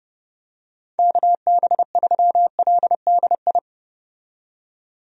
Morse code, heard as a keyed tone, about 30 wpm, 700 Hz. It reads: KB3LDI
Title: USA Call Signs - 6 Characters - 30wpm